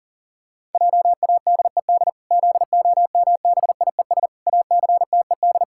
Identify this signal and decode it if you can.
Morse code, heard as a keyed tone, about 40 wpm, 700 Hz. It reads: JADED ZOMBIES ACTED
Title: Pangram Challenge v2 - 40wpm